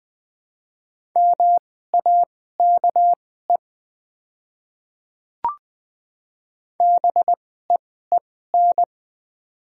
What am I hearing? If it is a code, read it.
Morse code, heard as a keyed tone, about 20 wpm, spaced as for 14 wpm, 700 Hz. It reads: MAKE BEEN
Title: 2x Character Spacing - Top 200 Words - 20wpm